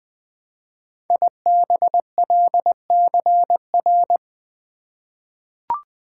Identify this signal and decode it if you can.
Morse code, heard as a keyed tone, about 20 wpm, 700 Hz. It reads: IBLCR
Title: Code Groups - 5 Letter:Numbers - 20wpm